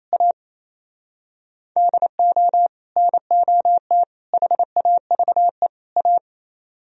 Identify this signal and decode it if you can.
Morse code, heard as a keyed tone, about 34 wpm, 700 Hz. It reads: A DO NOT HAVE A
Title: Speed-Racing - Sets of 4 Words 28wpm